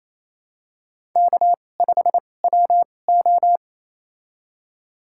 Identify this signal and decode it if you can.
Morse code, heard as a keyed tone, about 28 wpm, spaced as for 20 wpm, 700 Hz. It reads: K5WO
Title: USA Call Signs - 2x Character Spacing - 28wpm